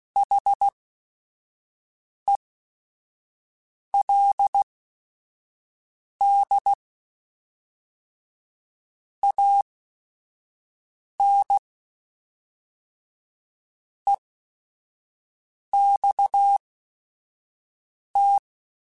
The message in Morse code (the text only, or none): HELD AN EXT